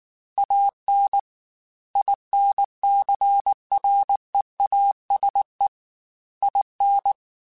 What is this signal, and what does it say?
Morse code, heard as a keyed tone, about 19 wpm, 800 Hz. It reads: AN INCREASE IN
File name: Technical Subjects / Technical Subjects-mp3s / XC @18